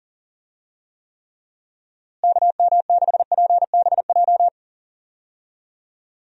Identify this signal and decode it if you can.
Morse code, heard as a keyed tone, about 40 wpm, 700 Hz. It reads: KM6PBJ
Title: USA Call Signs - 6 Characters - Repeated 2x - 40wpm